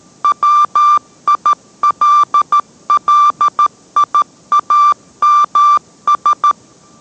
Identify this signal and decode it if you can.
Morse code, heard as a keyed tone, about 13 wpm, 1180 Hz. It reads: WILLIAMS